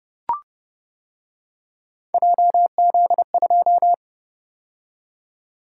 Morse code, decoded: JZ2

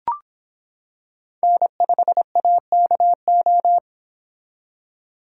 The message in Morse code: N5AKO

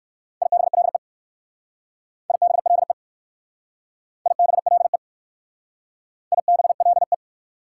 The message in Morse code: IBLE IBLE IBLE IBLE

